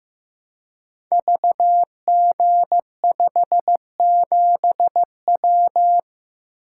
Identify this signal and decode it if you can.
Morse code, heard as a keyed tone, about 15 wpm, 700 Hz. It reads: VG57W
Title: Code Groups - 5 Letter:Numbers - 15wpm